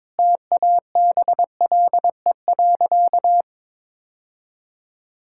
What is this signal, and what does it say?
Morse code, heard as a keyed tone, about 21 wpm, 700 Hz. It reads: TABLE.
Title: Sentences from Top 1000 Words - Review Part 5 - 22wpm